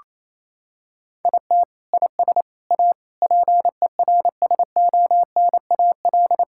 Morse code, heard as a keyed tone, about 28 wpm, 700 Hz. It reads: IT IS A PERSONAL
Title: Sentences from Top 1000 Words - Review Part 5 - 28wpm